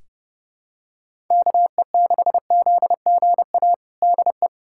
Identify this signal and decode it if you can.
Morse code, heard as a keyed tone, about 30 wpm, 700 Hz. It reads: KE6ZGA DE
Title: USA Callsign DE Callsign 30wpm